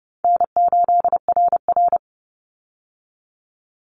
Morse code, heard as a keyed tone, about 30 wpm, 700 Hz. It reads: N8RR
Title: Call Signs - 30wpm